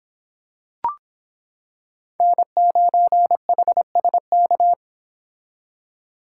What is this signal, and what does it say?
Morse code, heard as a keyed tone, about 25 wpm, 700 Hz. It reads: N9HSK